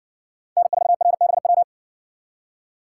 Morse code, heard as a keyed tone, about 57 wpm, 700 Hz. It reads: N4RBW